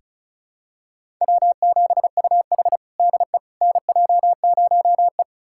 Morse code, heard as a keyed tone, about 34 wpm, 700 Hz. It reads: W7UH DE NJ0E
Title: USA Callsign DE Callsign - 4 Characters - 35wpm